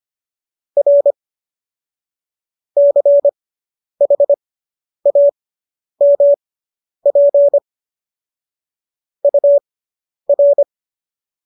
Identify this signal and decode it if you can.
Morse code, heard as a keyed tone, about 25 wpm, spaced as for 10 wpm, 570 Hz. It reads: R CHAMP UR
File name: qso005 10